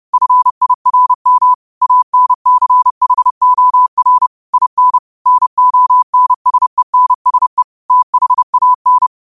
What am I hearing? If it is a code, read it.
Morse code, heard as a keyed tone, about 30 wpm, 1000 Hz. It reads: FIRM ANCHOR IN NONSENSE THAN